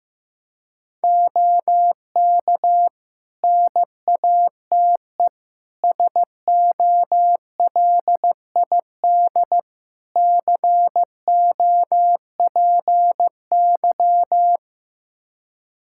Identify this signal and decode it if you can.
Morse code, heard as a keyed tone, about 16 wpm, 700 Hz. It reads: OK NATE SOLID COPY
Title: Practice QSO - Exchange 3 - More Info - 15wpm